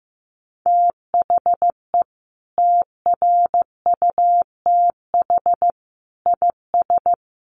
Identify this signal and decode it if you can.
Morse code, heard as a keyed tone, about 15 wpm, 700 Hz. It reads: THE TRUTH IS
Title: Sentences from Top 1500 Words 15wpm